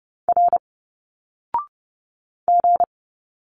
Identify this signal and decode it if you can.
Morse code, heard as a keyed tone, about 30 wpm, 700 Hz. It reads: R G
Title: Single Letters 30wpm